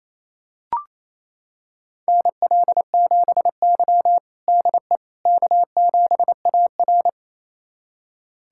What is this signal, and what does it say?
Morse code, heard as a keyed tone, about 28 wpm, 700 Hz. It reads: NL7Y DE K7AR
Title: USA Callsign DE Callsign - 4 Characters - 28wpm